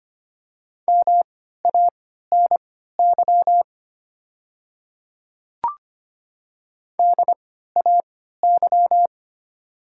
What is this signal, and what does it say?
Morse code, heard as a keyed tone, about 25 wpm, spaced as for 14 wpm, 700 Hz. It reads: MANY DAY
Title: 3x Character Spacing - Top 100 Words - 25wpm